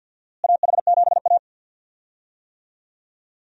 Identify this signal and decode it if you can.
Morse code, heard as a keyed tone, about 52 wpm, 700 Hz. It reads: AH7A
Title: USA Call Signs - 4 Characters - Repeated 2x - 50wpm